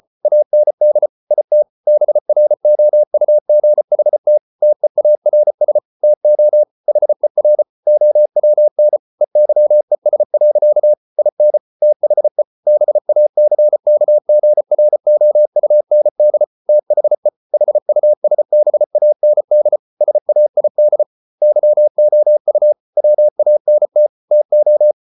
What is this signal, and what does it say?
Morse code, heard as a keyed tone, about 34 wpm, 600 Hz. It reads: AND IT BROUGHT TEARS TO HER OWN EYES. IN THE BACKGROUND THE HUSBAND SAID YOU WANT TO